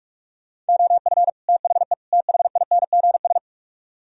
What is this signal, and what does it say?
Morse code, heard as a keyed tone, about 60 wpm, spaced as for 52 wpm, 700 Hz. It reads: OF THE THINGS
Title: Speed-Racing - Sets of 3 Words 45wpm